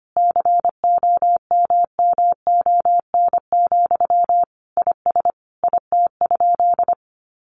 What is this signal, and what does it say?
Morse code, heard as a keyed tone, about 25 wpm, 700 Hz. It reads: COMMON, IS IT?
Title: Sentences from Top 700 Words 25wpm